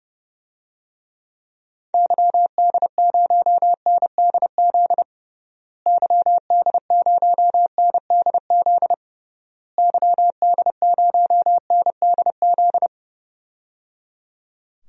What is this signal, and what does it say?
Morse code, heard as a keyed tone, about 30 wpm, 700 Hz. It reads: YD0NDZ YD0NDZ YD0NDZ
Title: DXCC-100 Call Signs - 6 Characters - Repeated 3x - 30wpm